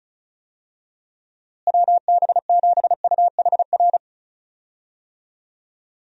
Morse code, heard as a keyed tone, about 34 wpm, 700 Hz. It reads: WB7UHR
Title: USA Call Signs - 6 Characters - Repeated 3x - 35wpm